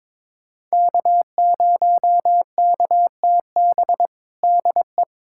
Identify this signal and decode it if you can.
Morse code, heard as a keyed tone, about 22 wpm, 700 Hz. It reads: K0KTB DE
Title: USA Callsign DE Callsign - 5 Characters - 22wpm